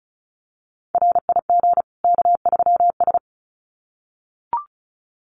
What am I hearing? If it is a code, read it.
Morse code, heard as a keyed tone, about 34 wpm, 700 Hz. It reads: RIG K3S
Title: Common QSO Phrases 35wpm